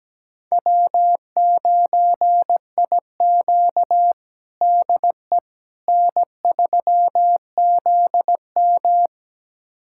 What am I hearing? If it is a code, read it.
Morse code, heard as a keyed tone, about 17 wpm, 700 Hz. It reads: W9IQ DE N3ZM
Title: USA Callsign DE Callsign - 4 Characters - 17wpm